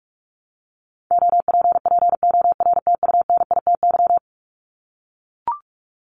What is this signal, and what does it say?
Morse code, heard as a keyed tone, about 44 wpm, 700 Hz. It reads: OPPORTUNITY
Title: Top 751-1000 Words 45wpm